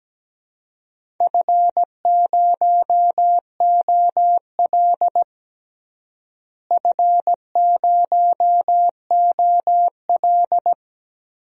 Morse code, heard as a keyed tone, about 17 wpm, 700 Hz. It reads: F0OL F0OL
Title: DXCC-100 Call Signs - 4 Characters - Repeated 3x - 17wpm